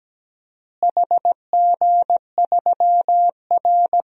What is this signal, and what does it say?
Morse code, heard as a keyed tone, about 17 wpm, 700 Hz. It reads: HG3R
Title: DXCC-100 Call Signs - 4 Characters - Repeated 2x - 17wpm